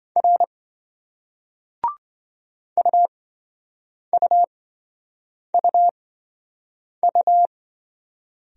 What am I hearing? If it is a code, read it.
Morse code, heard as a keyed tone, about 28 wpm, spaced as for 7 wpm, 700 Hz. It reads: R UUUU